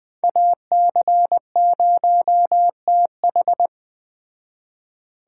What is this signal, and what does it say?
Morse code, heard as a keyed tone, about 20 wpm, 700 Hz. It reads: AC0TH
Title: Speed Racing - USA 2x2 Call Signs 17wpm